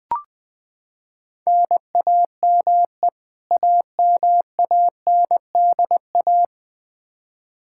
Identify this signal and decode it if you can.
Morse code, heard as a keyed tone, about 20 wpm, 700 Hz. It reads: NAME AMANDA
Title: Practice QSO - Exchange 1 - Beginning - 20wpm